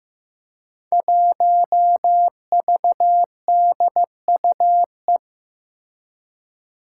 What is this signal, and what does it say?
Morse code, heard as a keyed tone, about 15 wpm, 700 Hz. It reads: 1VDUE
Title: Five Challenging Letter-Number Combinations 15wpm